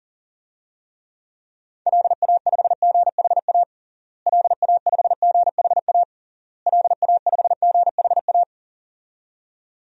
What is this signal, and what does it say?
Morse code, heard as a keyed tone, about 37 wpm, 700 Hz. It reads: LA5GHA LA5GHA LA5GHA
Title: DXCC-100 Call Signs - 6 Characters - Repeated 3x - 40wpm